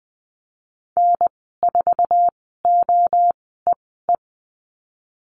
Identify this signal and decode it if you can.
Morse code, heard as a keyed tone, about 20 wpm, spaced as for 15 wpm, 700 Hz. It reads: N4OEE
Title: 069 Lesson 12g - Letter d - Call Signs 20wpm